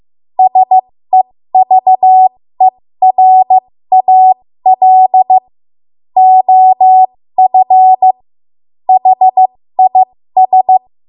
Morse code, heard as a keyed tone, about 15 wpm, spaced as for 13 wpm, 750 Hz. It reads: SEVERAL OF HIS
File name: Morse / 13WPM